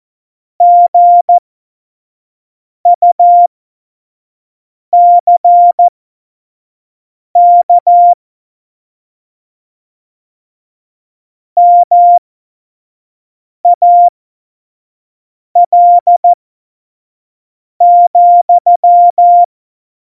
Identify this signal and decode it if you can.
Morse code, heard as a keyed tone, about 14 wpm, spaced as for 5 wpm, 700 Hz. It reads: GUCK MAL,
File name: Morsekurs / Witze / Witz 13. November 2018, 25 F70